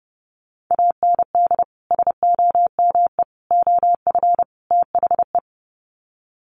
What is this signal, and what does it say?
Morse code, heard as a keyed tone, about 30 wpm, 700 Hz. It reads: AND SOME OF THE
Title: Sets of 4 Words 30wpm